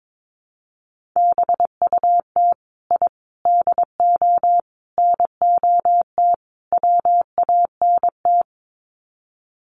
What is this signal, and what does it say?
Morse code, heard as a keyed tone, about 22 wpm, 700 Hz. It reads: BUT I DO NOT WANT
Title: Sets of 5 Words 22wpm